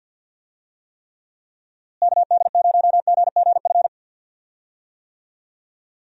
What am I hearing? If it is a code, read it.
Morse code, heard as a keyed tone, about 50 wpm, 700 Hz. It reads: KD0GGF
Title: USA Call Signs - 6 Characters - Repeated 2x - 50wpm